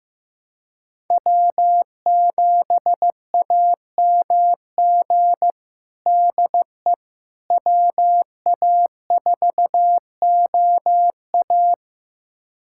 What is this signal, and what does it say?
Morse code, heard as a keyed tone, about 15 wpm, 700 Hz. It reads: W7AMG DE WA4OA
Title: USA Callsign DE Callsign - 5 Characters - 15wpm